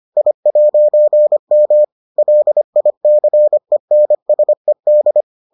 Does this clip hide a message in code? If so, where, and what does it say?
Yes: Morse code from 0.2 to 5.2 s, heard as a keyed tone, about 25 wpm, 600 Hz. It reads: I'M LICENSED